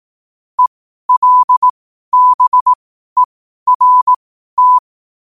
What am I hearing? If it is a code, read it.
Morse code, heard as a keyed tone, about 18 wpm, spaced as for 12 wpm, 1000 Hz. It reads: ELBERT
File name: qotdfiles / 12wpm